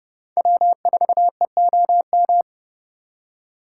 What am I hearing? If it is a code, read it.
Morse code, heard as a keyed tone, about 37 wpm, 700 Hz. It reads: W4EOM
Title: Speed Racing - USA 1x3 Call Signs 30wpm